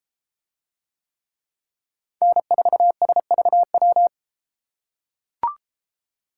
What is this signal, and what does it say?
Morse code, heard as a keyed tone, about 32 wpm, 700 Hz. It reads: N4SVW